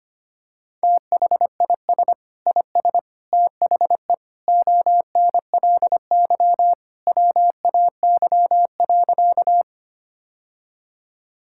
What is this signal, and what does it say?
Morse code, heard as a keyed tone, about 25 wpm, 700 Hz. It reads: THIS IS THE ONLY WAY.
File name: practice / 25wpm / Sentences with 2-Word Phrases - 25wpm